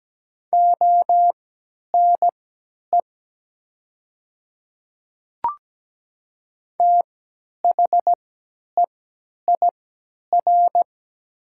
Morse code, heard as a keyed tone, about 17 wpm, spaced as for 10 wpm, 700 Hz. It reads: ONE THEIR
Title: 3x Character Spacing - Top 100 Words - 17wpm